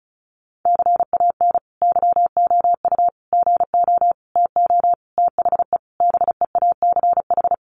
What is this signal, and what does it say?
Morse code, heard as a keyed tone, about 34 wpm, 700 Hz. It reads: CAN YOU GO TO THE BEACH
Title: Sentences from Top 1500 Words 35wpm